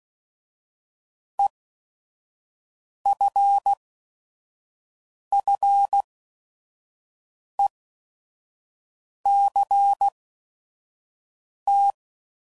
Morse code, heard as a keyed tone, about 16 wpm, spaced as for 5 wpm, 800 Hz. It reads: EFFECT
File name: Prefixes / By First Letter / Es